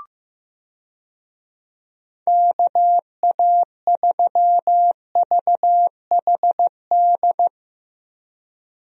Morse code, heard as a keyed tone, about 16 wpm, 700 Hz. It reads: KA3VHD
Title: USA Call Signs - 6 Characters - Repeated 2x - 15wpm